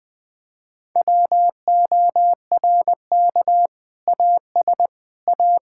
Morse code, heard as a keyed tone, about 21 wpm, 700 Hz. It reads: WORK AS A
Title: Sentences with 2-Word Phrases - Spoken Beginning - 20wpm